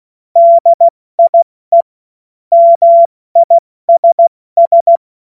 Morse code, heard as a keyed tone, about 16 wpm, 680 Hz. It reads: DIE MISS
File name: Morsekurs / Witze / Witz 02 April 2024 70 F80